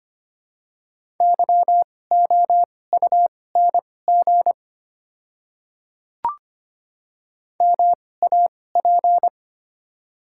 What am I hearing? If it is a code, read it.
Morse code, heard as a keyed tone, about 25 wpm, spaced as for 18 wpm, 700 Hz. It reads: YOUNG MAP
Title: 2x Character Spacing - Top 500 Words - 25wpm